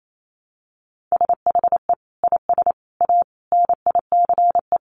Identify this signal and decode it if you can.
Morse code, heard as a keyed tone, about 28 wpm, 700 Hz. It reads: SHE IS A NICE
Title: Sentences from Top 1100 Words 28wpm